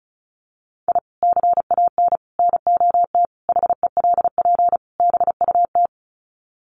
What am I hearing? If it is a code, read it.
Morse code, heard as a keyed tone, about 34 wpm, 700 Hz. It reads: I CAN NOT HELP BUT
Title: Sets of 5 Words Encore2 35wpm